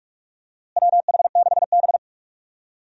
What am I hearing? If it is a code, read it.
Morse code, heard as a keyed tone, about 41 wpm, 700 Hz. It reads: WH6B
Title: USA Call Signs - 4 Characters - Repeated 2x - 45wpm